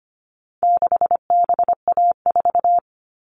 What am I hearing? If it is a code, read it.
Morse code, heard as a keyed tone, about 25 wpm, 700 Hz. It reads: 6BA4
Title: All Cut Numbers - 4-Digit 25wpm